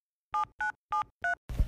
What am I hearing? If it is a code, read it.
DTMF, heard as telephone touch tones, tones of about 120 ms. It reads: 7973